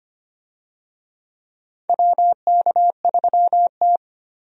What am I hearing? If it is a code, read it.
Morse code, heard as a keyed tone, about 25 wpm, 700 Hz. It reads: WK3T
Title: USA Call Signs - 4 Characters - 25wpm